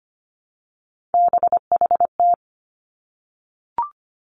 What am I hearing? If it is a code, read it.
Morse code, heard as a keyed tone, about 25 wpm, 700 Hz. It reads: BHT
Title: Three Challenging Letter-Number Combinations 25wpm